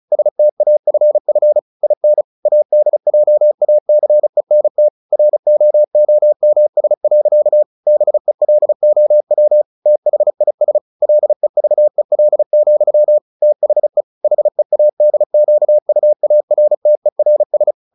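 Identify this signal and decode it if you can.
Morse code, heard as a keyed tone, about 34 wpm, 600 Hz. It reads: STAFF IN ADJACENT ROOMS. BELOW THIS LEVEL, THE HEADQUARTERS
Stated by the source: Morse practice at 35 wpm from CW med Gnister, sending Ken Follett's novel Never in English.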